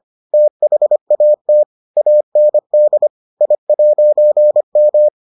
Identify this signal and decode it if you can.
Morse code, heard as a keyed tone, about 25 wpm, 600 Hz. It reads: THAT AND I'M